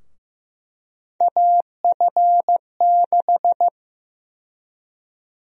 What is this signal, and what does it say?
Morse code, heard as a keyed tone, about 15 wpm, 700 Hz. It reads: AF6
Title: USA Call Sign Prefixes - 15wpm